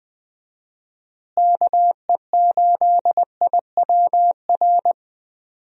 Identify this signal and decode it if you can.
Morse code, heard as a keyed tone, about 21 wpm, 700 Hz. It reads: KE8IWR
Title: USA Call Signs - 6 Characters - Repeated 2x - 20wpm